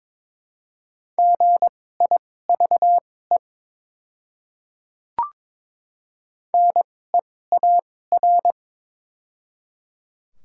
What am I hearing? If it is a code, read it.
Morse code, heard as a keyed tone, about 22 wpm, spaced as for 16 wpm, 700 Hz. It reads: GIVE NEAR
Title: 2x Character Spacing - Top 500 Words - 22wpm